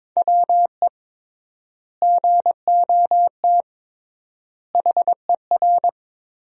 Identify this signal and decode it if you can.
Morse code, heard as a keyed tone, about 22 wpm, 700 Hz. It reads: WE GOT HER
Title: 3x Word-Spacing - Sentences from Top 400 Words - 22wpm